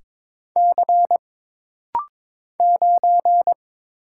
Morse code, heard as a keyed tone, about 22 wpm, 700 Hz. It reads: C 9
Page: This rapid-fire timing is geared to push your ICR (Instant Character Recognition) skills.